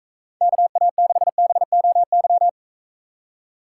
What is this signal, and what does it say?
Morse code, heard as a keyed tone, about 44 wpm, 700 Hz. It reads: KA6BOY